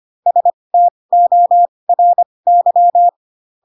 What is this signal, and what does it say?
Morse code, heard as a keyed tone, about 25 wpm, spaced as for 20 wpm, 700 Hz. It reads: STORY
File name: Pinocchio / Pinocchio ch4 15 25 ik1hge